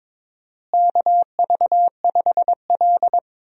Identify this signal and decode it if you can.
Morse code, heard as a keyed tone, about 22 wpm, 700 Hz. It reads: KV5L
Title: USA Call Signs - 4 Characters - Repeated 3x - 22wpm